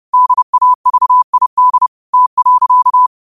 Morse code, heard as a keyed tone, about 30 wpm, 1000 Hz. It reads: DAVID T.
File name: qotdfiles / 30wpm